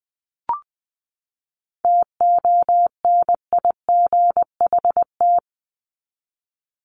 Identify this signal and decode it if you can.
Morse code, heard as a keyed tone, about 20 wpm, 700 Hz. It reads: TONIGHT
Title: Top 501-750 Words 20wpm